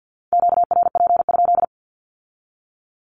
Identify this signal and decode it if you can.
Morse code, heard as a keyed tone, about 50 wpm, 700 Hz. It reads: QRP?
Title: Q-Codes with Spoken Meaning - 50wpm